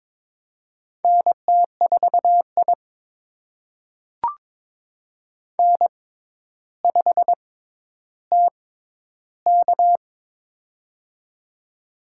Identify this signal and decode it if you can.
Morse code, heard as a keyed tone, about 22 wpm, spaced as for 11 wpm, 700 Hz. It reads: NT4I N5TK
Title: USA Call Signs - 4 Characters - 6x Character Spacing - 22wpm